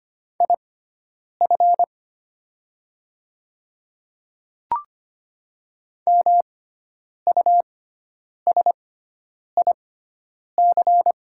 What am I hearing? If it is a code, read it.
Morse code, heard as a keyed tone, about 25 wpm, spaced as for 9 wpm, 700 Hz. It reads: IF MUSIC
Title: 6x Character Spacing - Top 300 Words - 25wpm